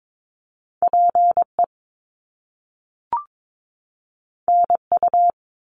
Morse code, heard as a keyed tone, about 22 wpm, 700 Hz. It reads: PE NU
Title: Canadian Provinces and Territories 22wpm